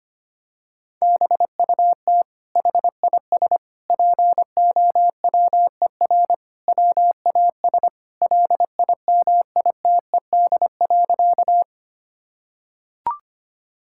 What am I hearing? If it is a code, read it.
Morse code, heard as a keyed tone, about 25 wpm, 700 Hz. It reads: BUT HIS POWER WAS LIMITED.
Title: Sentences from Top 1000 Words - Review Part 6 - 25wpm